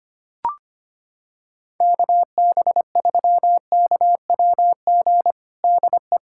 Morse code, heard as a keyed tone, about 25 wpm, 700 Hz. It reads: KB3KWG DE
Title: USA Callsign DE Callsign 25wpm